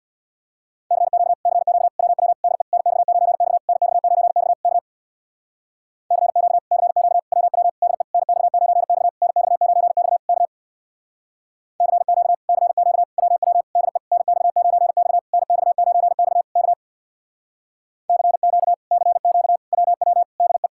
Morse code, heard as a keyed tone, about 63 wpm, 700 Hz. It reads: CQ CQ WW DE NX9X NX9X K CQ CQ WW DE NX9X NX9X K CQ CQ WW DE NX9X NX9X K CQ CQ WW DE